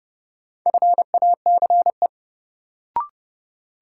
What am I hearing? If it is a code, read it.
Morse code, heard as a keyed tone, about 30 wpm, 700 Hz. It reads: FACE
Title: Sentences from Top 300 Words - New Word Drill - 30wpm